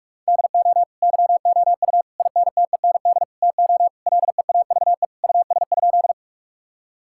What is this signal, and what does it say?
Morse code, heard as a keyed tone, about 47 wpm, 700 Hz. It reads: DO YOU INTEND TO LEAVE US?
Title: Sentences from Top 1600 Words 45wpm